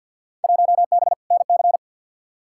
Morse code, heard as a keyed tone, about 51 wpm, 700 Hz. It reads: 1B NC
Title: Field Day Exchanges - 50wpm